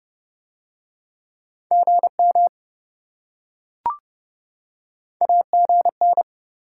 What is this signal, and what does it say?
Morse code, heard as a keyed tone, about 27 wpm, 700 Hz. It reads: GM AGN